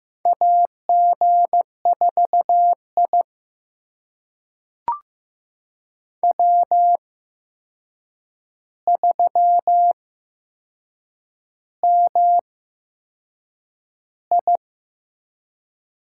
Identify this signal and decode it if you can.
Morse code, heard as a keyed tone, about 15 wpm, spaced as for 6 wpm, 700 Hz. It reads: AG4I W3MI
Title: USA Call Signs - 4 Characters - 8x Character Spacing - 15wpm